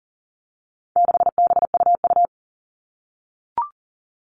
Morse code, heard as a keyed tone, about 40 wpm, 700 Hz. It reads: 6BUU
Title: All Cut Numbers - 4-Digit 40wpm